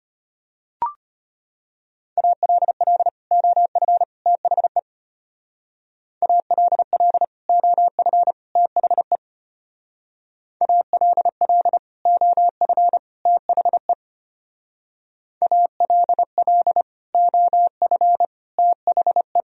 Morse code, heard as a keyed tone, about 31 wpm, 700 Hz. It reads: ALL OF THE ALL OF THE ALL OF THE ALL OF THE
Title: Speed-Racing - Sets of 3 Words 25wpm